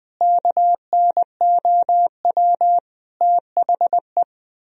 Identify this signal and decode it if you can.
Morse code, heard as a keyed tone, about 20 wpm, 700 Hz. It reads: KNOW THE
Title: Sentences from Top 1000 Words - Review Part 3 - 20wpm